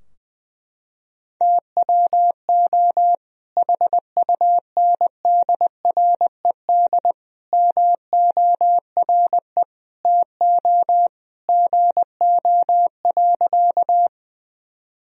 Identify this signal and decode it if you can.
Morse code, heard as a keyed tone, about 20 wpm, 700 Hz. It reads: TWO HUNDRED MORE TO GO.
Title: Sentences from Top 1000 Words - Review Part 2 - 20wpm